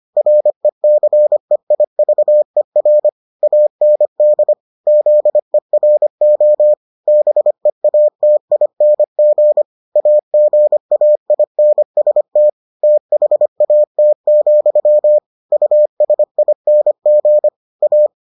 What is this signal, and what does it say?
Morse code, heard as a keyed tone, about 25 wpm, 600 Hz. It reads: RECEIVER AND ZERO BEATING AGAINST THAT, USING A